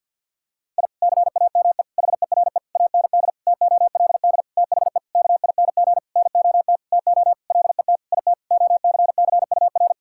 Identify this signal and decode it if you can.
Morse code, heard as a keyed tone, about 50 wpm, 700 Hz. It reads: I CAME HERE AND TOLD THE KING NOT TO LET IT OCCUR
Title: Sentences from Top 1000 Words - Review Part 6 - 50wpm